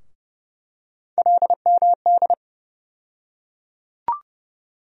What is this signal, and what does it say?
Morse code, heard as a keyed tone, about 30 wpm, 700 Hz. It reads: LMD
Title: USA Call Sign Postfixes - 3 Letters 30wpm